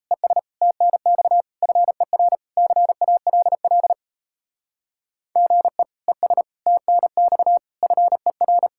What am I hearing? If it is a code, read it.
Morse code, heard as a keyed tone, about 34 wpm, 700 Hz. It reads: ES TNX FER CALL GE ES TNX FER
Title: Speed-Racing Common QSO Phrases - 28wpm